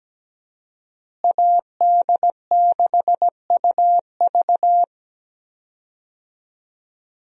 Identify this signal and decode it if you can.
Morse code, heard as a keyed tone, about 17 wpm, 700 Hz. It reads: AD6UV
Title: USA Call Signs - 5 Characters - Repeated 3x - 17wpm